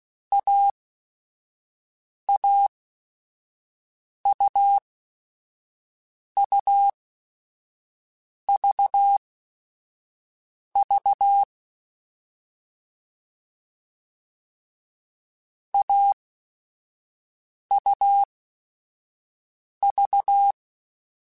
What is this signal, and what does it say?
Morse code, heard as a keyed tone, about 16 wpm, spaced as for 5 wpm, 800 Hz. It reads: AAUUVV AUV
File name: Short Runs / Short Runs-mp3s / AUV AUV4 @5